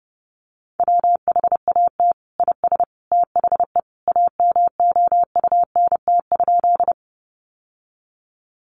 Morse code, heard as a keyed tone, about 28 wpm, 700 Hz. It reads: WHAT IS THE AMOUNT?
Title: Sentences from Top 700 Words 30wpm